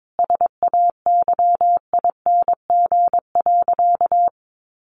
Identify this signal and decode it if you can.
Morse code, heard as a keyed tone, about 22 wpm, 700 Hz. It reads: SAYING.
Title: A Sound Of Thunder 22wpm